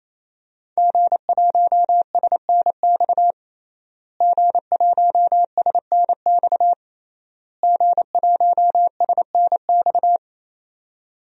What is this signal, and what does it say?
Morse code, heard as a keyed tone, about 27 wpm, 700 Hz. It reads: G1SNX G1SNX G1SNX
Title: DXCC-100 Call Signs - 5 Characters - Repeated 3x - 28wpm